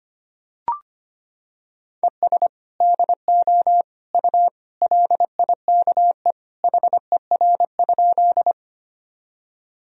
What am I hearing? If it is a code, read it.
Morse code, heard as a keyed tone, about 25 wpm, 700 Hz. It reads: ES DO U LIKE HER?
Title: Sentences from Top 100 Words and Abbreviations 25wpm